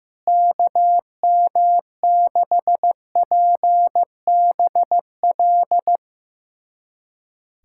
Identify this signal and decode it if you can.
Morse code, heard as a keyed tone, about 15 wpm, 700 Hz. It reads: KM6PBL
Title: USA Callsign DE Callsign - 6 Characters - 15wpm